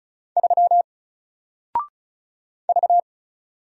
Morse code, heard as a keyed tone, about 35 wpm, 700 Hz. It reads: 3 V
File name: Single-Letter-Number-35wpm